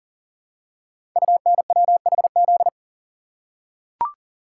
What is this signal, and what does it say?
Morse code, heard as a keyed tone, about 40 wpm, 700 Hz. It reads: UNWHZ
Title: Code Groups - 5 Letters - 40wpm